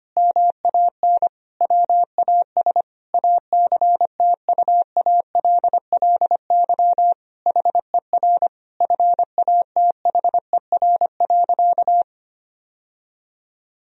Morse code, heard as a keyed tone, about 25 wpm, 700 Hz. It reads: MAN WAS ACTUALLY HER FATHER.
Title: Sentences from Top 1000 Words - Review Part 4 - 25wpm